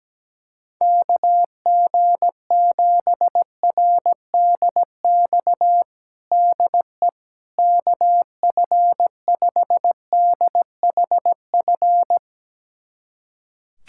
Morse code, heard as a keyed tone, about 17 wpm, 700 Hz. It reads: KG7RDX DE KF5DHF
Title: USA Callsign DE Callsign - 6 Characters - 17wpm